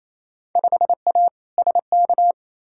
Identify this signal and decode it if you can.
Morse code, heard as a keyed tone, about 28 wpm, 700 Hz. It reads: 5A SK
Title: Field Day Exchanges Encore - 28wpm